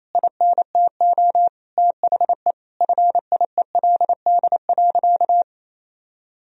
Morse code, heard as a keyed tone, about 28 wpm, 700 Hz. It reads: INTO THE FIELD.
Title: Sentences from Top 1000 Words - Review Part 2 - 28wpm